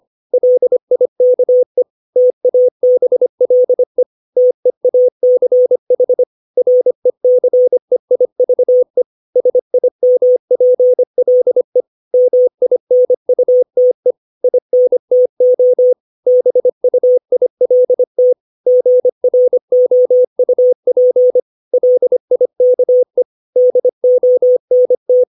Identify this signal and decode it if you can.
Morse code, heard as a keyed tone, about 25 wpm, 500 Hz. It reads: LIKE TABLE TEACH RECEIVE SIMPLE MINUTE INTO BUILT GROUP LIKE DONT
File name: 632 words / 632 Words 25WPM